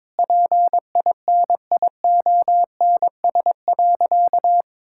Morse code, heard as a keyed tone, about 22 wpm, 700 Hz. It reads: PINIONS.
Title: Pangram Challenge v2 - 22wpm